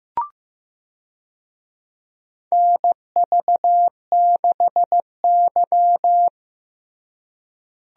Morse code, heard as a keyed tone, about 15 wpm, 700 Hz. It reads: NV6Y
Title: USA Call Signs - 4 Characters - Repeated 3x - 15wpm